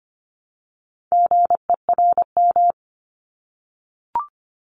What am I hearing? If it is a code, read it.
Morse code, heard as a keyed tone, about 25 wpm, 700 Hz. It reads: GERM
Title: Multi-letter Phonemes 25wpm